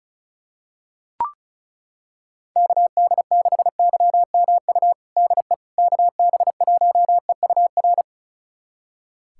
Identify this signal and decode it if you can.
Morse code, heard as a keyed tone, about 36 wpm, 700 Hz. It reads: KD6YMU DE KB1EUR
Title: USA Callsign DE Callsign - 6 Characters - 35wpm